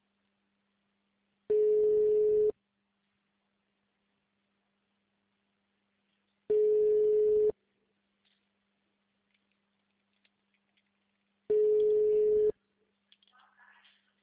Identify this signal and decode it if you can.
Ringback tone (Europe, 425 Hz), with speech.
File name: out- -2001- - - .26608